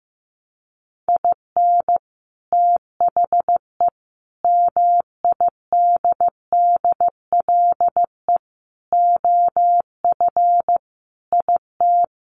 Morse code, heard as a keyed tone, about 15 wpm, 700 Hz. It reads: IN THE MIDDLE OF IT
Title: Sets of 5 Words Encore 15wpm